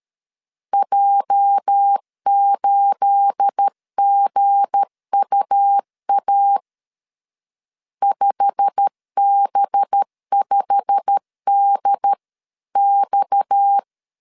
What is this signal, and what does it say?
Morse code, heard as a keyed tone, about 13 wpm, 800 Hz. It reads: J8GUA 5B5DX
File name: CW / DET / 1 9 1500C S B 12wpm 3819